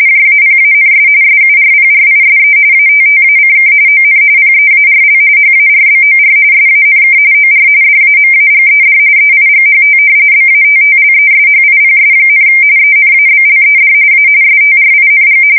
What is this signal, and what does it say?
RTTY, about 45.45 baud. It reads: JAPANS SURPRISE ATTACK ON PEARL HARBOR SANK OR DAMAGED EIGHT BATTLESHIPS, ENABLED A STRING
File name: fsk170 / battleshipnj